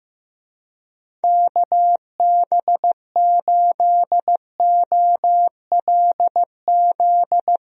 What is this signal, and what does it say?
Morse code, heard as a keyed tone, about 15 wpm, 700 Hz. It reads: KB8OLZ
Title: USA Call Signs - 6 Characters - Repeated 3x - 15wpm